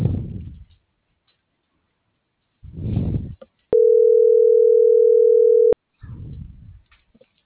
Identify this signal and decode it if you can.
Ringback tone (North America), 440 + 480 Hz.
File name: CENTRAL MULTIMODAL / out viridiana.juarez - .82511